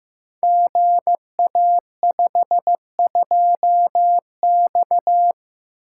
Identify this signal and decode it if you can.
Morse code, heard as a keyed tone, about 15 wpm, 700 Hz. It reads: GA52X
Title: Code Groups - 5 Letter:Numbers - 15wpm